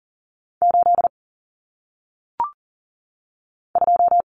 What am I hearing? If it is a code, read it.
Morse code, heard as a keyed tone, about 40 wpm, 700 Hz. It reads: 8 2
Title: Common Cut Numbers - 1-Digit 40wpm